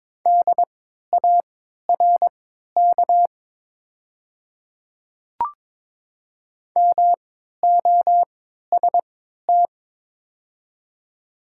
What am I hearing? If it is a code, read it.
Morse code, heard as a keyed tone, about 22 wpm, spaced as for 12 wpm, 700 Hz. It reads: DARK MOST
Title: 3x Character Spacing - Top 500 Words - 22wpm